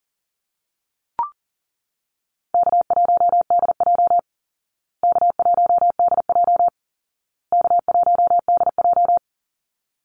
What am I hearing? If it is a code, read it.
Morse code, heard as a keyed tone, about 40 wpm, 700 Hz. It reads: K1DJ K1DJ K1DJ
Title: Easy Call Signs Repeated 3x - 40wpm